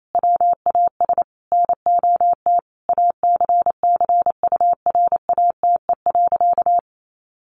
Morse code, heard as a keyed tone, about 28 wpm, 700 Hz. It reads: WAS NOT ACCURATE.
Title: Sentences from Top 1200 Words 28wpm